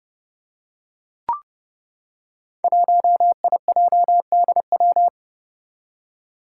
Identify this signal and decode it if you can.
Morse code, heard as a keyed tone, about 30 wpm, 700 Hz. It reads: 1IJDW
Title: Five Challenging Letter-Number Combinations 30wpm